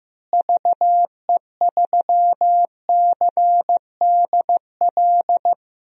Morse code, heard as a keyed tone, about 15 wpm, 700 Hz. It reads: VE3CDL
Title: DXCC-100 Call Signs - 6 Characters - Repeated 2x - 15wpm